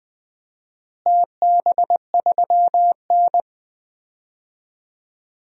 Morse code, heard as a keyed tone, about 20 wpm, 700 Hz. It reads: TB3N
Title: DXCC-100 Call Signs - 4 Characters - Repeated 2x - 20wpm